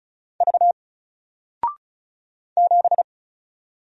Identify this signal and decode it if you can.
Morse code, heard as a keyed tone, about 33 wpm, 700 Hz. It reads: V 7
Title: Single Letter-Number Alt 35wpm